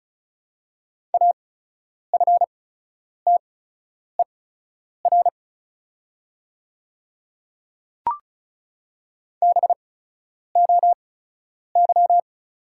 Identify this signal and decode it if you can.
Morse code, heard as a keyed tone, about 34 wpm, spaced as for 10 wpm, 690 Hz. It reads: AFTER BOY
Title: 8x Character Spacing - Top 200 Words - 35wpm